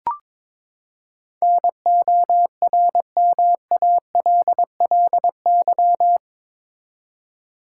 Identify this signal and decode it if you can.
Morse code, heard as a keyed tone, about 22 wpm, 700 Hz. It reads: NORMALLY
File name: practice / 22wpm / Eight Letter Words - 22wpm